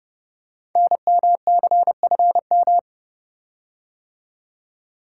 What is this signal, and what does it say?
Morse code, heard as a keyed tone, about 30 wpm, 700 Hz. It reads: NMCFM
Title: Code Groups - 5 Letters - 30wpm